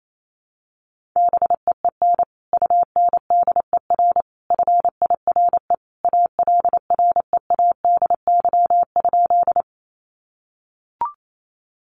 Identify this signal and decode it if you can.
Morse code, heard as a keyed tone, about 28 wpm, 700 Hz. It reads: BEEN UNDER FIRE ALREADY?
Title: Sentences from Top 700 Words 28wpm